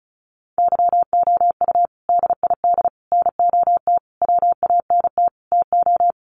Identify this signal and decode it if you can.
Morse code, heard as a keyed tone, about 34 wpm, 700 Hz. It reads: YOU DID NOT WANT TO
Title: Sets of 5 Words Encore2 35wpm